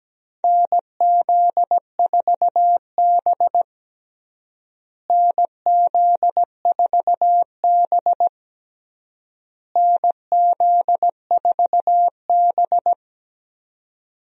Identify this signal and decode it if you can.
Morse code, heard as a keyed tone, about 17 wpm, 700 Hz. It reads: NZ4B NZ4B NZ4B
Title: USA Call Signs - 4 Characters - Repeated 3x - 17wpm